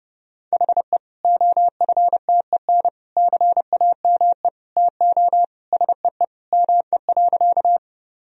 Morse code, heard as a keyed tone, about 30 wpm, 700 Hz. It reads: HE OFTEN CAME TO SEE ME.
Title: Sentences from Top 1000 Words - Review Part 1 - 30wpm